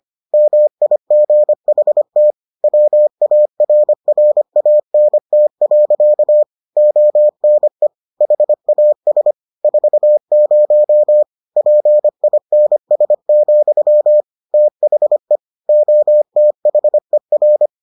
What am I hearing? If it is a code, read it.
Morse code, heard as a keyed tone, about 25 wpm, 600 Hz. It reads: MIGHT WARRANT. ONE HAS 40 PINS, THE OTHER